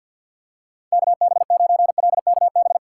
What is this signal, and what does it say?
Morse code, heard as a keyed tone, about 47 wpm, 700 Hz. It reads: KB9LKB
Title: USA Call Signs - 6x Character Spacing - 50wpm